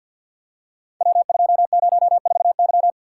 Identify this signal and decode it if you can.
Morse code, heard as a keyed tone, about 51 wpm, 700 Hz. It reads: W204Y
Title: Code Groups - 5 Letter:Numbers - 50wpm